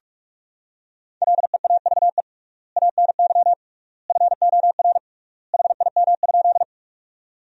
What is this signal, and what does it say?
Morse code, heard as a keyed tone, about 45 wpm, 700 Hz. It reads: LEAVE ANY FOR HIM?
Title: 3x Word-Spacing - Sentences from Top 400 Words - 45wpm